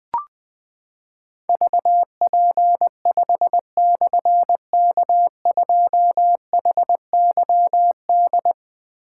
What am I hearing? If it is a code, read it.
Morse code, heard as a keyed tone, about 21 wpm, 700 Hz. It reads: VP5/K2HYD
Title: Difficult Call Signs 20wpm